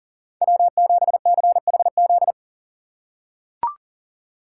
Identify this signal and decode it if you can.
Morse code, heard as a keyed tone, about 38 wpm, 700 Hz. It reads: W7CHZ
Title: USA Call Signs - 4x Character Spacing - 40wpm